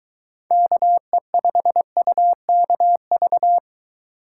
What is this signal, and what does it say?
Morse code, heard as a keyed tone, about 23 wpm, 700 Hz. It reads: KE5UKV